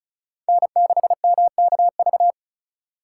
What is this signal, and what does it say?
Morse code, heard as a keyed tone, about 44 wpm, 700 Hz. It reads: N6MKV
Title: Speed Racing - USA 1x3 Call Signs 35wpm